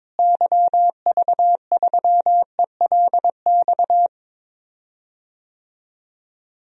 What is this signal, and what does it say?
Morse code, heard as a keyed tone, about 22 wpm, 700 Hz. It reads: YV3ELX
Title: DXCC-100 Call Signs - 6 Characters - Repeated 2x - 22wpm